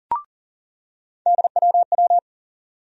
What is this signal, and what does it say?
Morse code, heard as a keyed tone, about 40 wpm, 700 Hz. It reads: DWW
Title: USA Call Sign Postfixes - 3 Letters 40wpm